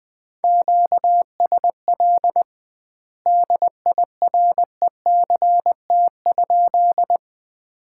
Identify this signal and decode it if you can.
Morse code, heard as a keyed tone, about 20 wpm, 700 Hz. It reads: QSL DIRECT?
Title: Common QSO Phrases - 2x Word-Spacing - 20wpm